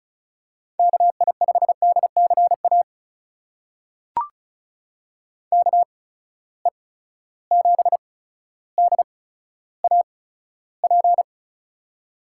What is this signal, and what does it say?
Morse code, heard as a keyed tone, about 36 wpm, spaced as for 15 wpm, 700 Hz. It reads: KI5DCA KE7DAP
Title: USA Call Signs - 6 Characters - 8x Character Spacing - 35wpm